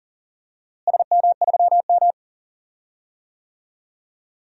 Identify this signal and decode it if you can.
Morse code, heard as a keyed tone, about 38 wpm, 700 Hz. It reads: SM3M
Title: DXCC-100 Call Signs - 4 Characters - Repeated 2x - 40wpm